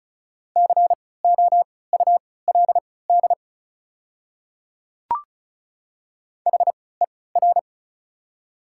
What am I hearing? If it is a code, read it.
Morse code, heard as a keyed tone, about 34 wpm, spaced as for 20 wpm, 690 Hz. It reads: COULD HER
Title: 3x Character Spacing - Top 100 Words - 35wpm